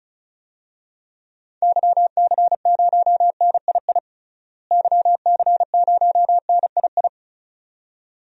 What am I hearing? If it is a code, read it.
Morse code, heard as a keyed tone, about 36 wpm, 700 Hz. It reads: YC0NII YC0NII
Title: DXCC-100 Call Signs - 6 Characters - Repeated 2x - 35wpm